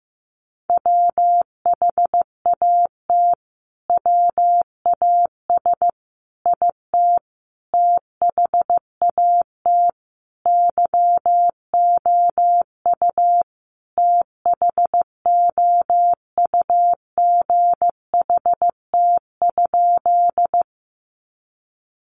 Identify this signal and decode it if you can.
Morse code, heard as a keyed tone, about 15 wpm, 700 Hz. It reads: WHAT WAS IT THAT YOU THOUGHT?
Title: Sentences from Top 300 Words 15wpm